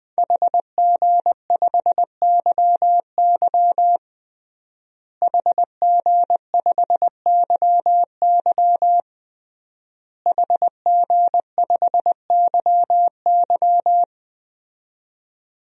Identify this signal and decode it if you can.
Morse code, heard as a keyed tone, about 20 wpm, 700 Hz. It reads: HG5YY HG5YY HG5YY
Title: DXCC-100 Call Signs - 5 Characters - Repeated 3x - 20wpm